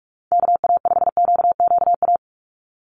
Morse code, heard as a keyed tone, about 44 wpm, 700 Hz. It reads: KA5QQA
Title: NA SOTA Chaser Callsigns 45wpm